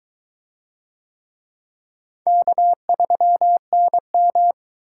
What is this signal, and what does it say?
Morse code, heard as a keyed tone, about 21 wpm, 700 Hz. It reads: K3NM